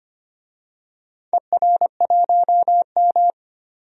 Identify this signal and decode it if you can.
Morse code, heard as a keyed tone, about 25 wpm, 700 Hz. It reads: ER1M
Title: DXCC-100 Call Signs - 4 Characters - 25wpm